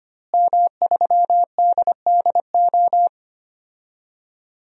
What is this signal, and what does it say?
Morse code, heard as a keyed tone, about 25 wpm, 700 Hz. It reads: M3DDO
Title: DXCC-100 Call Signs - 5 Characters - Repeated 3x - 25wpm